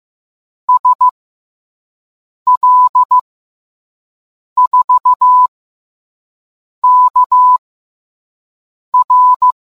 Morse code, heard as a keyed tone, about 15 wpm, 1000 Hz. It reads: S L 4 K R